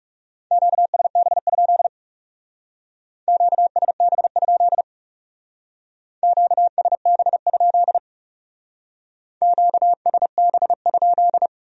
Morse code, heard as a keyed tone, about 40 wpm, 700 Hz. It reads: QSB? QSB? QSB? QSB?